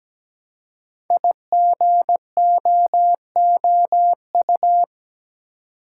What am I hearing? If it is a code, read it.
Morse code, heard as a keyed tone, about 17 wpm, 700 Hz. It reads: IGOOU
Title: Code Groups - 5 Letters - 17wpm